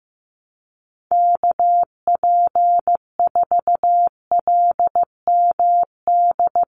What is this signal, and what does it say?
Morse code, heard as a keyed tone, about 15 wpm, 700 Hz. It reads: KP4LMD
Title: USA Call Signs 15wpm